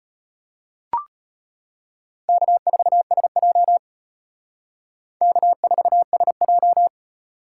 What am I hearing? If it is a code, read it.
Morse code, heard as a keyed tone, about 32 wpm, 700 Hz. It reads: K4SJ K4SJ